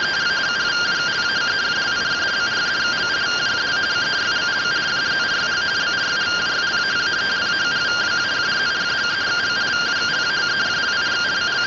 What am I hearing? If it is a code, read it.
Two-tone teleprinter signal, about 170 Hz shift.